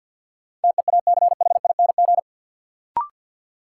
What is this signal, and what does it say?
Morse code, heard as a keyed tone, about 53 wpm, 700 Hz. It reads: TEACHING
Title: Sentences from Top 1100 Words - New Word Drill Alt - 50wpm